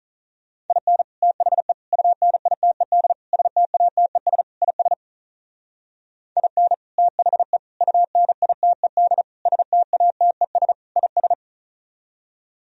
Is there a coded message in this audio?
Yes: Morse code from 0.7 to 11.3 s, heard as a keyed tone, about 44 wpm, 700 Hz. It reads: IN THE UNITED STATES IS IN THE UNITED STATES IS